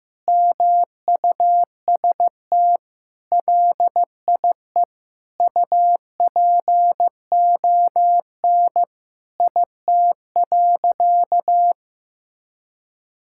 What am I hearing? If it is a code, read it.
Morse code, heard as a keyed tone, about 15 wpm, 700 Hz. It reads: MUST LIE UPON IT.
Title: Sentences from Top 1600 Words 15wpm